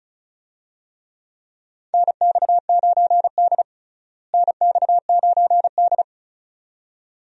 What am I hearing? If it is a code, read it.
Morse code, heard as a keyed tone, about 36 wpm, 700 Hz. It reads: NX9D NX9D
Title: USA Call Signs - 4 Characters - Repeated 2x - 35wpm